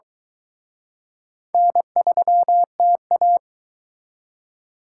Morse code, heard as a keyed tone, about 23 wpm, 700 Hz. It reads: N3TA